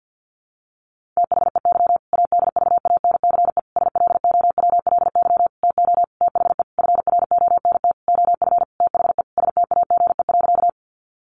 Morse code, heard as a keyed tone, about 50 wpm, 700 Hz. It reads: THEY ADVANCE SLOWLY TO THE FRONT OF THE STAGE.
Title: Sentences from Top 1400 Words 50wpm